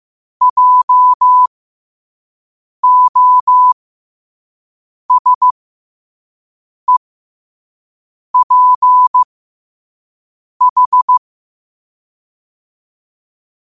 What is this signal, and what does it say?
Morse code, heard as a keyed tone, about 15 wpm, spaced as for 5 wpm, 1000 Hz. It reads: JOSEPH
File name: qotdfiles / 5wpm